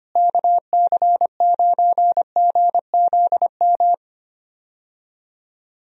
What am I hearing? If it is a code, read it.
Morse code, heard as a keyed tone, about 25 wpm, 700 Hz. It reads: KC9GZM
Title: USA Call Signs - 6 Characters - Repeated 3x - 25wpm